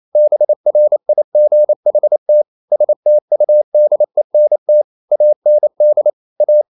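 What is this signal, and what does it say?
Morse code, heard as a keyed tone, about 28 wpm, 600 Hz. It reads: BRIGHT STUDENT AND A
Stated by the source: Morse code practice at 28 wpm, text of Ken Follett's novel Never in English, from CW med Gnister.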